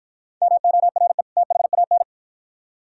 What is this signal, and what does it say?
Morse code, heard as a keyed tone, about 53 wpm, 700 Hz. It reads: MORE THAN